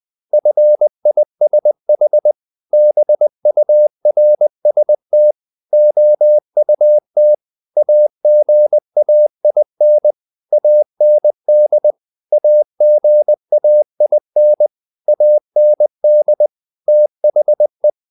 Morse code, heard as a keyed tone, about 20 wpm, 600 Hz. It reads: FISH BURST OUT AGAIN AND AGAIN AND THE
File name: the old man and the sea / the old man and the sea part 1 20wpm